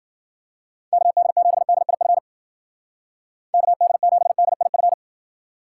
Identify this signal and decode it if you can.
Morse code, heard as a keyed tone, about 53 wpm, 700 Hz. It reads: KD7DIF KD7DIF